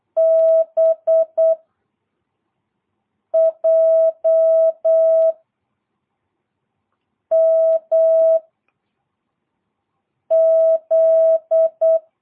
Morse code, heard as a keyed tone, about 8 wpm, 650 Hz. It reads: B J M Z
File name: jh8cbh / 2022cw